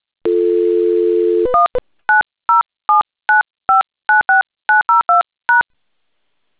DTMF, heard as telephone touch tones, tones of about 150 ms. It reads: 19079596902#